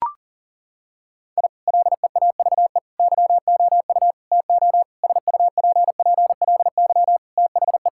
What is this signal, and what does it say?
Morse code, heard as a keyed tone, about 37 wpm, 700 Hz. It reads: I LEAVE YOU TO SUPPLY THE
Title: Sentences from Top 1000 Words - Review Part 5 - 40wpm